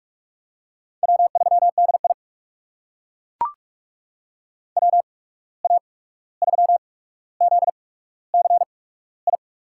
Morse code, heard as a keyed tone, about 42 wpm, spaced as for 17 wpm, 700 Hz. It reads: W3DI WA3ZCI
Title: USA Call Signs - 8x Character Spacing - 45wpm